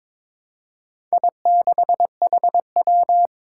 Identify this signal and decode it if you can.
Morse code, heard as a keyed tone, about 22 wpm, 700 Hz. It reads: I6HW
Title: Four Challenging Letter-Number Combinations 22wpm